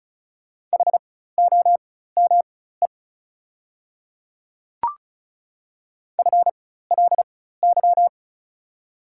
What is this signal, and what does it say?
Morse code, heard as a keyed tone, about 36 wpm, spaced as for 16 wpm, 690 Hz. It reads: HOME FLY
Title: 4x Character Spacing - Top 500 Words - 35wpm